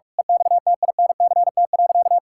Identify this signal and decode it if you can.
Morse code, heard as a keyed tone, about 41 wpm, 700 Hz. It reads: EXTINCT.
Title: Pangram Challenge v2 - 45wpm